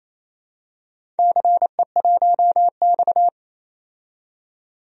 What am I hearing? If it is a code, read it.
Morse code, heard as a keyed tone, about 28 wpm, 700 Hz. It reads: CE1X
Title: DXCC-100 Call Signs - 4 Characters - 28wpm